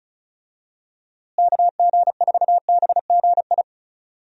Morse code, heard as a keyed tone, about 34 wpm, 700 Hz. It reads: KG4BGI
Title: USA Call Signs - 6 Characters - Repeated 2x - 35wpm